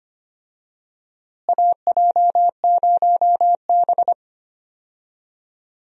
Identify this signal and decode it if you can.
Morse code, heard as a keyed tone, about 25 wpm, 700 Hz. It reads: AJ0B
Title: USA Call Signs - 4 Characters - Repeated 3x - 25wpm